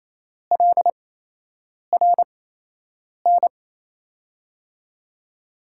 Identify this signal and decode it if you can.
Morse code, heard as a keyed tone, about 27 wpm, spaced as for 8 wpm, 700 Hz. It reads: LRN
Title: USA Call Signs - 8x Character Spacing - 28wpm